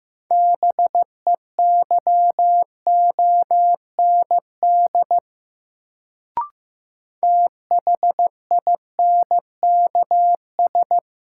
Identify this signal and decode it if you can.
Morse code, heard as a keyed tone, about 15 wpm, 700 Hz. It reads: BEYOND THINKS
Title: Sentences from Top 1000 Words - Review Part 6 - New Word Drill - 15wpm